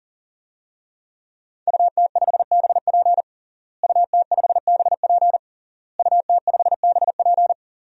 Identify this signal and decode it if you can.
Morse code, heard as a keyed tone, about 40 wpm, 700 Hz. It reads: UT5BP UT5BP UT5BP
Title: DXCC-100 Call Signs - 5 Characters - Repeated 3x - 40wpm